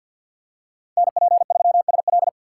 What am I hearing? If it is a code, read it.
Morse code, heard as a keyed tone, about 52 wpm, 700 Hz. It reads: NP3SL
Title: USA Call Signs - 5 Characters - 2x Character Spacing - 50wpm